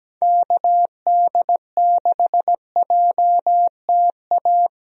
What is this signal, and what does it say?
Morse code, heard as a keyed tone, about 17 wpm, 700 Hz. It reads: KD6JTA
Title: USA Call Signs - 6 Characters - Repeated 3x - 17wpm